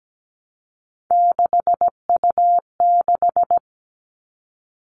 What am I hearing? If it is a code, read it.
Morse code, heard as a keyed tone, about 17 wpm, 700 Hz. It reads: 6U6
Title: All Cut Numbers - 3-Digit 17wpm